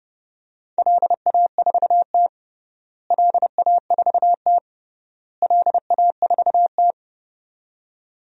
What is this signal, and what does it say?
Morse code, heard as a keyed tone, about 32 wpm, 700 Hz. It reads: LA4T LA4T LA4T
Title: DXCC-100 Call Signs - 4 Characters - Repeated 3x - 30wpm